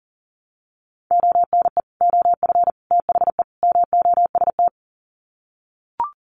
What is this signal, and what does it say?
Morse code, heard as a keyed tone, about 40 wpm, 700 Hz. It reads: ONE OF THE MOST
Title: Sets of 4 Words 40wpm